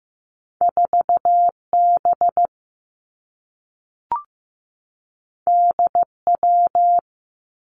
Morse code, heard as a keyed tone, about 15 wpm, 700 Hz. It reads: 4 B D W
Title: Two Challenging Letter-Number Combinations 15wpm